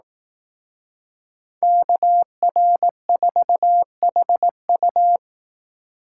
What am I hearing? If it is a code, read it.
Morse code, heard as a keyed tone, about 18 wpm, 700 Hz. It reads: KR4HU
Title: Speed Racing - USA 2x2 Call Signs 15wpm